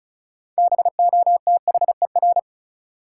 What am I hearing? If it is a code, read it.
Morse code, heard as a keyed tone, about 35 wpm, 700 Hz. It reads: BOTHER
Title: Six Letter Words - 35wpm